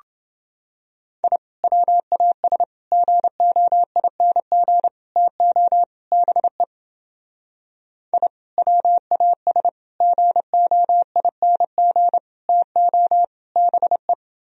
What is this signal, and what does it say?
Morse code, heard as a keyed tone, about 30 wpm, 700 Hz. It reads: I WAS GOING TO BE I WAS GOING TO BE